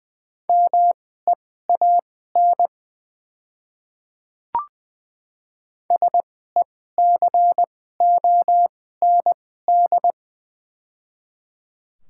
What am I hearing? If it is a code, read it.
Morse code, heard as a keyed tone, about 20 wpm, spaced as for 15 wpm, 700 Hz. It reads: MEAN SECOND
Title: 2x Character Spacing - Top 300 Words - 20wpm